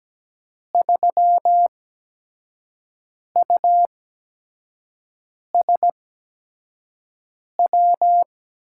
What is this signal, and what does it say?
Morse code, heard as a keyed tone, about 17 wpm, spaced as for 5 wpm, 700 Hz. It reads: 3USW
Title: USA Call Signs - 6 Characters - 8x Character Spacing - 17wpm